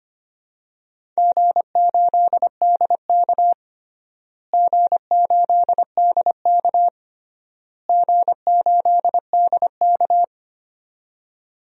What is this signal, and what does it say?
Morse code, heard as a keyed tone, about 25 wpm, 700 Hz. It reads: G8DK G8DK G8DK
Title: DXCC-100 Call Signs - 4 Characters - Repeated 3x - 25wpm